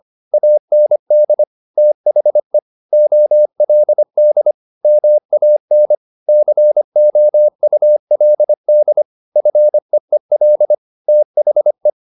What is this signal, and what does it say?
Morse code, heard as a keyed tone, about 25 wpm, 600 Hz. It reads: AND THE OLD MAN COULD FEEL THE